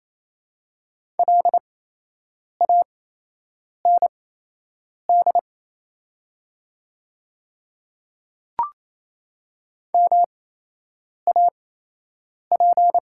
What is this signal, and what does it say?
Morse code, heard as a keyed tone, about 28 wpm, spaced as for 8 wpm, 700 Hz. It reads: LAND MAP
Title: 8x Character Spacing - Top 500 Words - 28wpm